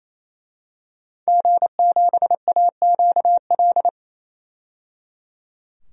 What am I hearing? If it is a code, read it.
Morse code, heard as a keyed tone, about 28 wpm, 700 Hz. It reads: G7AQL
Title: DXCC-100 Call Signs - 5 Characters - 28wpm